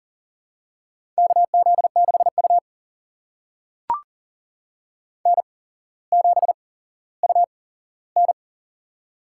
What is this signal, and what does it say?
Morse code, heard as a keyed tone, about 40 wpm, spaced as for 17 wpm, 700 Hz. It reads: KZ6U N7UN
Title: USA Call Signs - 4 Characters - 8x Character Spacing - 40wpm